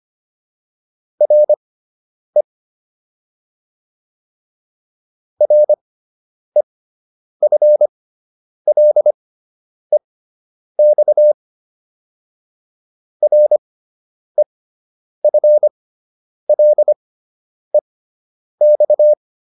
Morse code, heard as a keyed tone, about 25 wpm, spaced as for 9 wpm, 600 Hz. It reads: RE REFLEX REFLEX